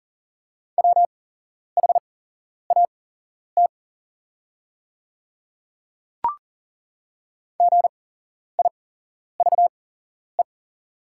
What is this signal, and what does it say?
Morse code, heard as a keyed tone, about 40 wpm, spaced as for 11 wpm, 690 Hz. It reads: WHAT GIVE
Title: 8x Character Spacing - Top 200 Words - 40wpm